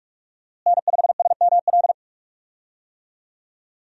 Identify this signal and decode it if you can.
Morse code, heard as a keyed tone, about 45 wpm, 700 Hz. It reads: N5SML
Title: USA Call Signs - 5 Characters - Repeated 2x - 45wpm